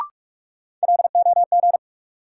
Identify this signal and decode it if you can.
Morse code, heard as a keyed tone, about 46 wpm, 700 Hz. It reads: LOG
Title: Sentences from Top 1400 Words - New Word Drill Alt - 45wpm